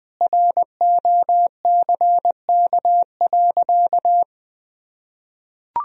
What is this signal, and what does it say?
Morse code, heard as a keyed tone, about 21 wpm, 700 Hz. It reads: ROCK.
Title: Sentences from Top 1000 Words - Review Part 1 - 20wpm